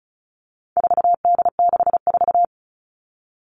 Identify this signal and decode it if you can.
Morse code, heard as a keyed tone, about 34 wpm, 700 Hz. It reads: 4D64
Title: All Cut Numbers - 4-Digit 35wpm